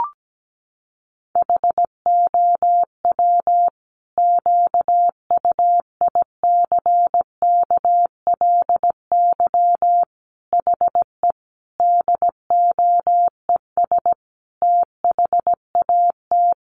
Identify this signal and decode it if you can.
Morse code, heard as a keyed tone, about 17 wpm, 700 Hz. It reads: HOW QUICKLY HE DOES THAT
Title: Sentences from Top 700 Words 17wpm